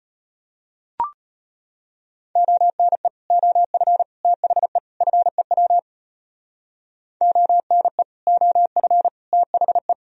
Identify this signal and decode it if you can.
Morse code, heard as a keyed tone, about 32 wpm, 700 Hz. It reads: ONE OF THE FEW ONE OF THE